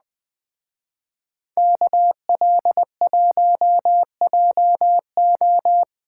Morse code, heard as a keyed tone, about 20 wpm, 700 Hz. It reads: KL1JO
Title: Speed Racing - USA 2x2 Call Signs 17wpm